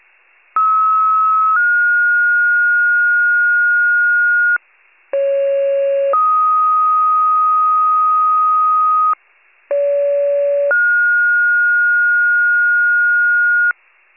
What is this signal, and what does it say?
Two-tone page, heard as a one-second tone, then a three-second tone, about 1290 then 1430 Hz, about 550 then 1250 Hz, about 570 then 1430 Hz, with noise. It is heard from 0.6 to 13.7 s.
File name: pager / 73 FIRE / wcalert